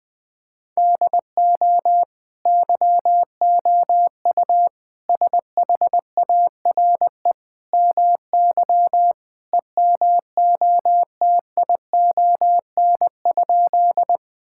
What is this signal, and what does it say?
Morse code, heard as a keyed tone, about 20 wpm, 700 Hz. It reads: DO YOU SHARE MY EMOTION?
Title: Sentences from Top 1600 Words 20wpm